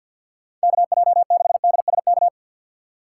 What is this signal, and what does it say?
Morse code, heard as a keyed tone, about 52 wpm, 700 Hz. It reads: KJ6DSK
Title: USA Call Signs - 6 Characters - 3x Character Spacing - 50wpm